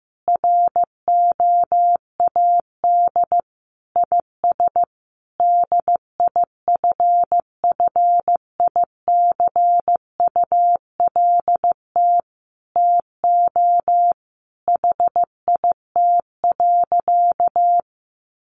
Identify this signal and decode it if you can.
Morse code, heard as a keyed tone, about 15 wpm, 700 Hz. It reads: ROAD IS DIFFICULT TO HIT.
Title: Sentences from Top 1200 Words 15wpm